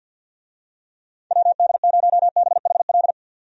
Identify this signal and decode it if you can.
Morse code, heard as a keyed tone, about 50 wpm, 700 Hz. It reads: WD0BHL
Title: USA Call Signs - 6 Characters - 50wpm